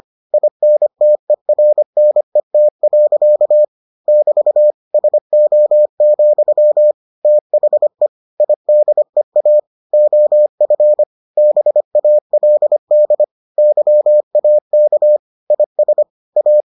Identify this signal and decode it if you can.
Morse code, heard as a keyed tone, about 25 wpm, 600 Hz. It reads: INTERNET. <BT> SO, THE IDEA OF BALD YAK IS A